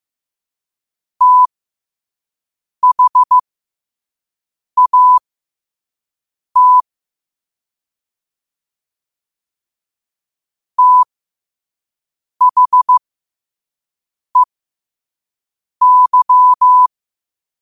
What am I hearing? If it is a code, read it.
Morse code, heard as a keyed tone, about 15 wpm, spaced as for 5 wpm, 1000 Hz. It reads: THAT THEY
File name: qotdfiles / 5wpm